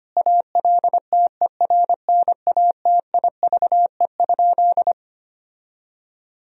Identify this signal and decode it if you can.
Morse code, heard as a keyed tone, about 25 wpm, 700 Hz. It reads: ALTERNATIVE?
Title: Sentences from Top 1000 Words - Review Part 4 - 25wpm